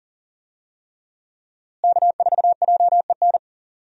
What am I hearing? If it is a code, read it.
Morse code, heard as a keyed tone, about 40 wpm, 700 Hz. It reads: K4JEN
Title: USA Call Signs - 5 Characters - Repeated 3x - 40wpm